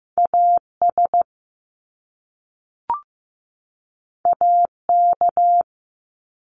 Morse code, heard as a keyed tone, about 15 wpm, 700 Hz. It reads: AS AK
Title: Top Two Letter Combinations in Words 15wpm